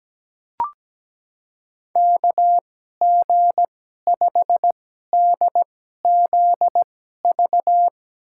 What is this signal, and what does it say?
Morse code, heard as a keyed tone, about 17 wpm, spaced as for 12 wpm, 700 Hz. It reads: KG5DZV
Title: USA Call Signs - 6 Characters - 2x Character Spacing - 17wpm